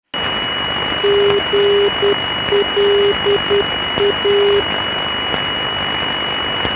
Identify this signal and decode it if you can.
Morse code, heard as a keyed tone, about 10 wpm, 400 Hz, with noise. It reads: GLA
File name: electronics / broadcast / GLA-375kHz